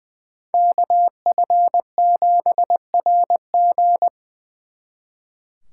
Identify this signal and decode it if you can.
Morse code, heard as a keyed tone, about 24 wpm, 700 Hz. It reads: KF7RG
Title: Speed Racing - USA 2x2 Call Signs 20wpm